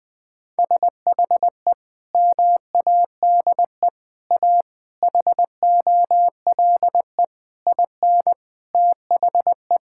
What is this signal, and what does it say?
Morse code, heard as a keyed tone, about 20 wpm, 700 Hz. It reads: SHE MADE A HOLE IN THE
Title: Sentences from Top 1600 Words 20wpm